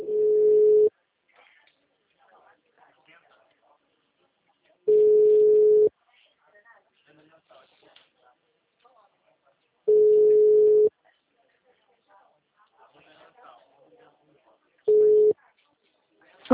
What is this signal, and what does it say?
Ringback tone (Europe, 425 Hz), with speech.